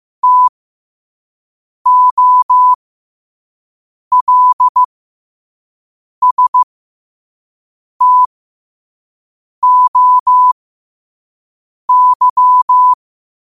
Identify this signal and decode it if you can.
Morse code, heard as a keyed tone, about 15 wpm, spaced as for 5 wpm, 1000 Hz. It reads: TOLSTOY